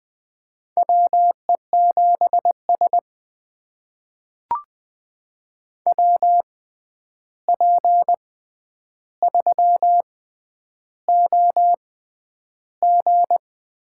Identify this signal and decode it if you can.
Morse code, heard as a keyed tone, about 20 wpm, spaced as for 10 wpm, 700 Hz. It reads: WE7S WP3OG
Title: USA Call Signs - 6x Character Spacing - 20wpm